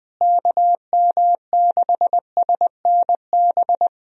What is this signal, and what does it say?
Morse code, heard as a keyed tone, about 20 wpm, 700 Hz. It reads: KM6SNB
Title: USA Call Signs - 6 Characters - Repeated 2x - 20wpm